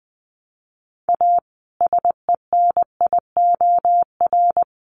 Morse code, heard as a keyed tone, about 20 wpm, 700 Hz. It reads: A SENIOR
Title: Sets of 5 Words Encore 20wpm